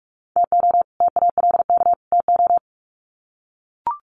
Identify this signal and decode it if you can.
Morse code, heard as a keyed tone, about 44 wpm, 700 Hz. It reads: TO TALK TO
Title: Sets of 3 Words 45wpm